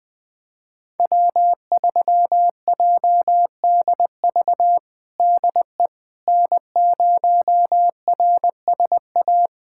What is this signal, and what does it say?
Morse code, heard as a keyed tone, about 20 wpm, 700 Hz. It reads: W3JDV DE N0RSA
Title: USA Callsign DE Callsign - 5 Characters - 20wpm